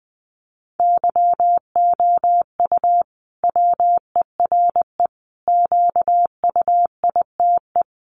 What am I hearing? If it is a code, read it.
Morse code, heard as a keyed tone, about 19 wpm, 700 Hz. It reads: YOU WERE QUITE
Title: Sentences from Top 1200 Words 20wpm